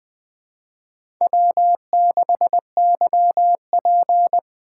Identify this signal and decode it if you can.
Morse code, heard as a keyed tone, about 20 wpm, 700 Hz. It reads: W6YP
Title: Speed Racing - USA Call Signs 17wpm